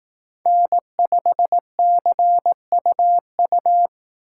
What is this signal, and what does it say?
Morse code, heard as a keyed tone, about 18 wpm, 700 Hz. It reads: N5CUU
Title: Speed Racing - USA Call Signs 15wpm